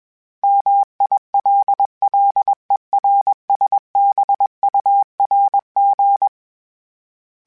Morse code, heard as a keyed tone, about 21 wpm, 800 Hz. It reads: MILLERSBURG
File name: fromSMs / FromSMs mp3s / Cities@20